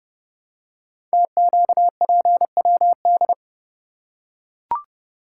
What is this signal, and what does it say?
Morse code, heard as a keyed tone, about 30 wpm, 700 Hz. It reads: TQPWD
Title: Code Groups - 5 Letters - 30wpm